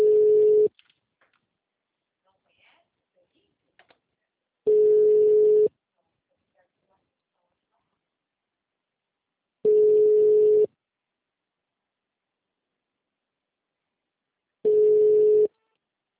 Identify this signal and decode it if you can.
Ringback tone (Europe, 425 Hz).
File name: out- -2041- - - .22628